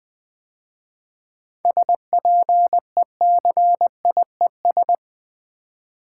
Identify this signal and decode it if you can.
Morse code, heard as a keyed tone, about 20 wpm, 700 Hz. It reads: SPECIES
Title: 7 Letter Words - First 3 Letters Given - 20wpm